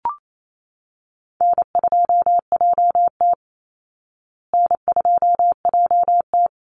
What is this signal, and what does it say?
Morse code, heard as a keyed tone, about 28 wpm, 700 Hz. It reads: N2JT N2JT
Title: Easy Call Signs Repeated 3x - 28wpm